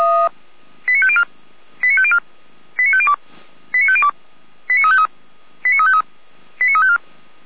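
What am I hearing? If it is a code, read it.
Five-tone selective call (ZVEI), heard as quick runs of five tones, tones of about 70 ms. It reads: 89504 89504 89502 89502 89353 89353 89354
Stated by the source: Received on an SDR; ZVEI five-tone code 89080.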